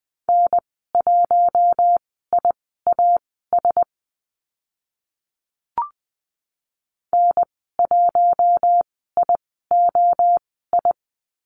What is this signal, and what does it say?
Morse code, heard as a keyed tone, about 20 wpm, spaced as for 14 wpm, 700 Hz. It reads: N1IAS N1IOI
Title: 042 Lesson 8g - Number 1 - Call Signs 20wpm